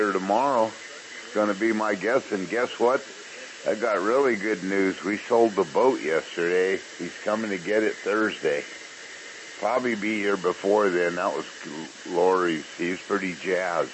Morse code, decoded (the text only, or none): none